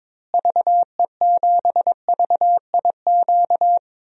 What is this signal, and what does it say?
Morse code, heard as a keyed tone, about 22 wpm, 700 Hz. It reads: VE7VIQ
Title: DXCC-100 Call Signs - 6 Characters - Repeated 2x - 22wpm